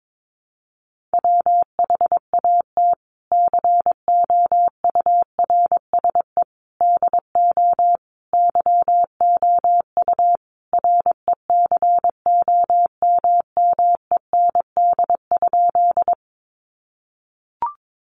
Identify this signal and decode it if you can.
Morse code, heard as a keyed tone, about 22 wpm, 700 Hz. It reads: WHAT COURSE DO YOU RECOMMEND?
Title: Sentences from Top 1200 Words 22wpm